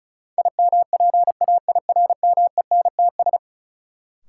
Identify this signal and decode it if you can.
Morse code, heard as a keyed tone, about 36 wpm, 700 Hz. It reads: IMPAIRMENTS
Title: Sets of 5 Words Encore - New Word Drill - 35wpm